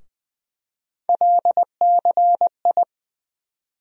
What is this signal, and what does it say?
Morse code, heard as a keyed tone, about 20 wpm, 700 Hz. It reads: LCI
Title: USA Call Sign Postfixes - 3 Letters 20wpm